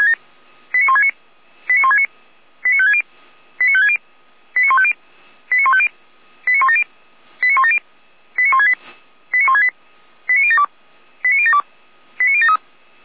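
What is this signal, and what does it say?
Five-tone selective call (ZVEI), heard as quick runs of five tones, tones of about 70 ms. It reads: 89169 89169 89560 89560 89150 89150 89170 89170 89168 89168 89072 89072 89073
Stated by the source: ZVEI five-tone call received by SDR, archived by fire brigade Feuerwehr Orlamünde.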